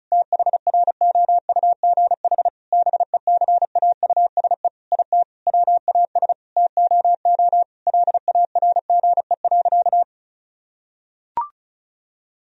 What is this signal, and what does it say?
Morse code, heard as a keyed tone, about 34 wpm, 700 Hz. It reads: THROUGH BECAUSE IT WAS TOO LARGE.
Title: Sentences from Top 1000 Words - Review Part 1 - 35wpm